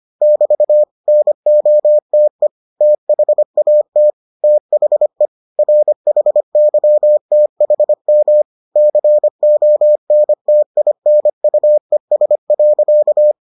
Morse code, heard as a keyed tone, about 25 wpm, 600 Hz. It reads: <BT> NOTE THAT THE RHYTHM CONTINUES.